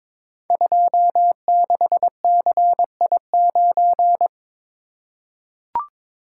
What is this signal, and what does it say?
Morse code, heard as a keyed tone, about 22 wpm, 700 Hz. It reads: 26CI9
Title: Code Groups - 5 Letter:Numbers - 22wpm